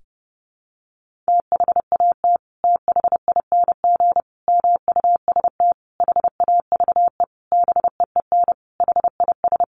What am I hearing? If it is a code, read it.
Morse code, heard as a keyed tone, about 30 wpm, 700 Hz. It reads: THAT THING MUST HAVE BEEN HIS
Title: Sentences from Top 1500 Words 30wpm